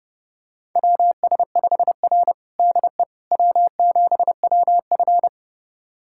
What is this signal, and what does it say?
Morse code, heard as a keyed tone, about 29 wpm, 700 Hz. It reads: WS5R DE W7WF
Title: USA Callsign DE Callsign - 4 Characters - 30wpm